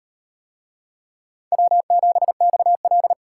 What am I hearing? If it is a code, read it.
Morse code, heard as a keyed tone, about 35 wpm, 700 Hz. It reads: W7XL